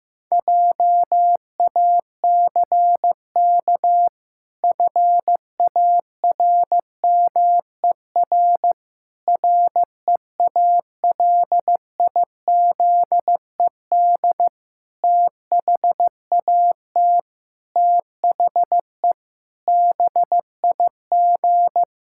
Morse code, heard as a keyed tone, about 15 wpm, 700 Hz. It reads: JACK FARMER REALIZED THAT THE BIG
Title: Pangram Challenge v2 - 15wpm